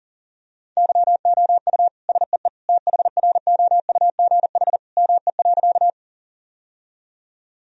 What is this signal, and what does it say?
Morse code, heard as a keyed tone, about 37 wpm, 700 Hz. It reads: YOU SEE THROUGH ME.
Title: Sentences from Top 1000 Words - Review Part 1 - 40wpm